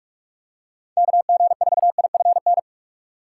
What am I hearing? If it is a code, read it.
Morse code, heard as a keyed tone, about 44 wpm, 700 Hz. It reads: K G 4 I F N
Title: USA Call Signs - 6 Characters - 6x Character Spacing - 45wpm